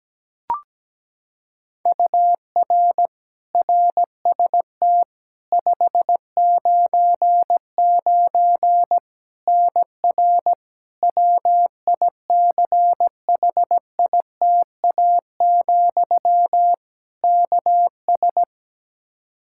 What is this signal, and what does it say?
Morse code, heard as a keyed tone, about 17 wpm, 700 Hz. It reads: UR RST 599 NR WICHITA, KS
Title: K1USN SST QSO Practice 17wpm